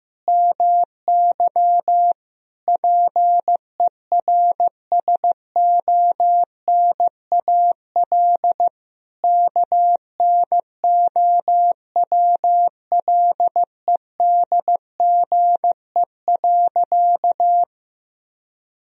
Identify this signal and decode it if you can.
Morse code, heard as a keyed tone, about 15 wpm, 700 Hz. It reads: MY PERSONAL KNOWLEDGE.
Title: Sentences from Top 1000 Words - Review Part 4 - 15wpm